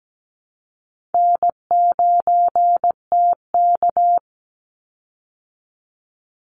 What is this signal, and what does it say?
Morse code, heard as a keyed tone, about 17 wpm, 700 Hz. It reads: N9TK
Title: Easy Call Signs Repeated 3x - 17wpm